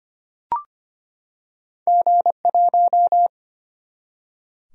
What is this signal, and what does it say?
Morse code, heard as a keyed tone, about 25 wpm, 700 Hz. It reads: G1
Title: Single-Letter Single-Number 25wpm